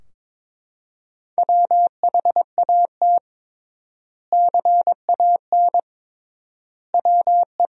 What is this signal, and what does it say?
Morse code, heard as a keyed tone, about 22 wpm, 700 Hz. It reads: WHAT CAN WE
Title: 3x Word-Spacing - Sentences from Top 100 Words - 22wpm